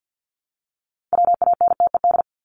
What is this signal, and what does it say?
Morse code, heard as a keyed tone, about 50 wpm, 700 Hz. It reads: WANTED
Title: ED Suffix - Top 100 Words with Base Words 50wpm